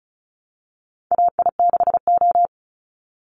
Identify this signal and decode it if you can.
Morse code, heard as a keyed tone, about 34 wpm, 700 Hz. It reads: AI6O
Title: Call Signs Repeated 3x - 35wpm